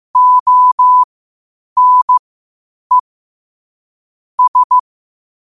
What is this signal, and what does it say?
Morse code, heard as a keyed tone, about 15 wpm, spaced as for 8 wpm, 1000 Hz. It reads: ONE S
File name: qotdfiles / 8wpm